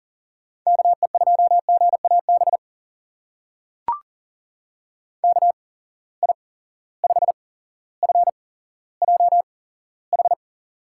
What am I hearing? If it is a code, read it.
Morse code, heard as a keyed tone, about 40 wpm, spaced as for 17 wpm, 700 Hz. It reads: KE2GAB KI5FJH
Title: USA Call Signs - 6 Characters - 8x Character Spacing - 40wpm